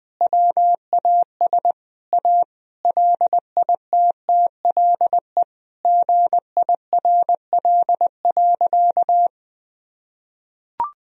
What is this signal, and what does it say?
Morse code, heard as a keyed tone, about 20 wpm, 700 Hz. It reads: WAS A LITTLE GIRL.
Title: Sentences from Top 1000 Words - Review Part 1 - 20wpm